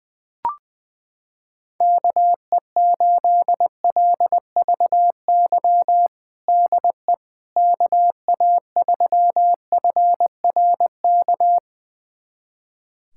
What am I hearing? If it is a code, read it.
Morse code, heard as a keyed tone, about 20 wpm, 700 Hz. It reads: KE8LVY DE KA3FRK
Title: USA Callsign DE Callsign - 6 Characters - 20wpm